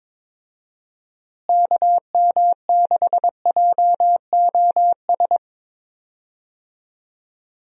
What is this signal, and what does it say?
Morse code, heard as a keyed tone, about 22 wpm, 700 Hz. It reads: KM6JOS
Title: USA Call Signs - 6 Characters - Repeated 3x - 22wpm